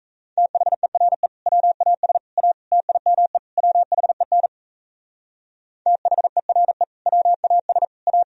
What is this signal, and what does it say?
Morse code, heard as a keyed tone, about 35 wpm, 700 Hz. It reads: THERE WAS A TIME WHEN THERE WAS A